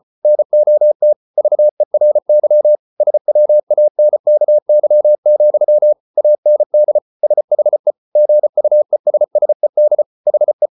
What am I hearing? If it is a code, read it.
Morse code, heard as a keyed tone, about 34 wpm, 600 Hz. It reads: NOT VERY SWANKY, AND SHE GUESSED HE